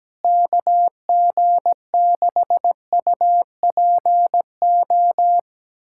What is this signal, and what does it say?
Morse code, heard as a keyed tone, about 17 wpm, 700 Hz. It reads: KG6UPO
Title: USA Callsign DE Callsign - 6 Characters - 17wpm